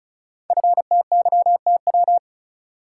Morse code, heard as a keyed tone, about 36 wpm, 690 Hz. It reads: FTYTW
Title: Code Groups - 5 Letters - 35wpm